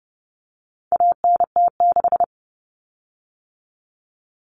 Morse code, heard as a keyed tone, about 30 wpm, 700 Hz. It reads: ANT6
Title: Common Cut Numbers - 4-Digit 30wpm